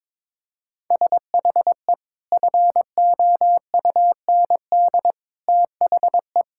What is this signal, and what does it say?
Morse code, heard as a keyed tone, about 21 wpm, 700 Hz. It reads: SHE FOUND THE
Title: Sentences from Top 1000 Words - Review Part 6 - 22wpm